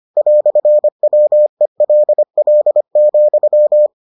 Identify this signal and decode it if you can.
Morse code, heard as a keyed tone, about 25 wpm, 600 Hz. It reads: "WELL,